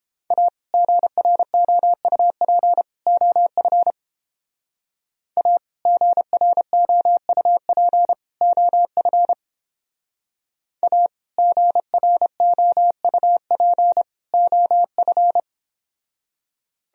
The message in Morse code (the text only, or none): A GROUP OF A GROUP OF A GROUP OF